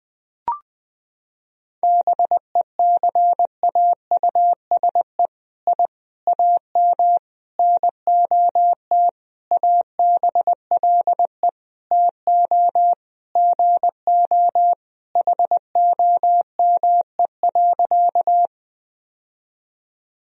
Morse code, heard as a keyed tone, about 20 wpm, 700 Hz. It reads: BECAUSE I AM NOT ABLE TO GO HOME.
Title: Sentences from Top 1000 Words - Review Part 3 - 20wpm